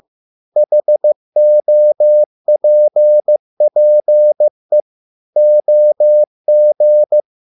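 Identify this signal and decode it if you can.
Morse code, heard as a keyed tone, about 15 wpm, 600 Hz. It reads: HOPPE OG
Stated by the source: Morse code at a slow 15 wpm from CW med Gnister.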